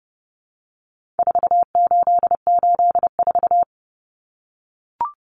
Morse code, heard as a keyed tone, about 30 wpm, 700 Hz. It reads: 4884
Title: Common Cut Numbers - 4-Digit 30wpm